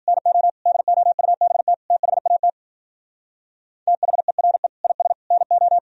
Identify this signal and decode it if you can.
Morse code, heard as a keyed tone, about 50 wpm, 700 Hz. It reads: NO DOUBT THAT THERE IS NO